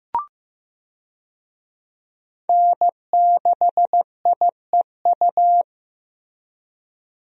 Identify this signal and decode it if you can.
Morse code, heard as a keyed tone, about 15 wpm, 700 Hz. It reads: N6IEU
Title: USA Call Signs - 5 Characters - Repeated 2x - 15wpm